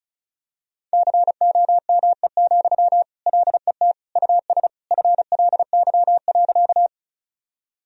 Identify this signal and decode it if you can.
Morse code, heard as a keyed tone, about 36 wpm, 700 Hz. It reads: COME, LET US FLY.
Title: Sentences from Top 1000 Words - Review Part 2 - 35wpm